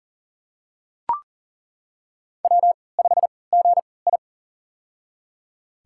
Morse code, heard as a keyed tone, about 37 wpm, spaced as for 22 wpm, 700 Hz. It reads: W5GI
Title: USA Call Signs - 4 Characters - 3x Character Spacing - 40wpm